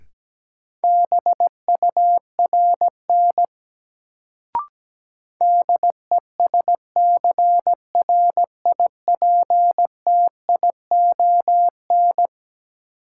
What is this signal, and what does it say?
Morse code, heard as a keyed tone, about 17 wpm, 700 Hz. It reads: BURN DESCRIPTION
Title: Sentences from Top 1300 Words - New Word Drill - 17wpm